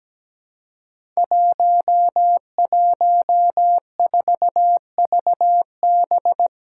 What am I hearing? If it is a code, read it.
Morse code, heard as a keyed tone, about 17 wpm, 700 Hz. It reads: 114VB
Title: Five Challenging Letter-Number Combinations 17wpm